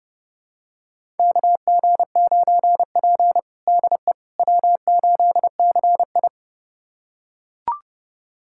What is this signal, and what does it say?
Morse code, heard as a keyed tone, about 32 wpm, 700 Hz. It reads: KG9P DE W8CI
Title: USA Callsign DE Callsign - 4 Characters - 30wpm